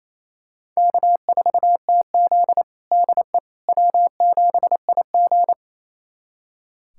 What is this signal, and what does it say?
Morse code, heard as a keyed tone, about 28 wpm, 700 Hz. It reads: K4TZ DE W7IG
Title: USA Callsign DE Callsign - 4 Characters - 28wpm